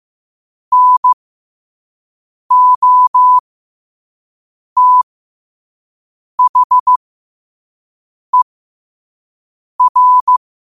Morse code, heard as a keyed tone, about 15 wpm, spaced as for 5 wpm, 1000 Hz. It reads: NOTHER
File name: qotdfiles / 5wpm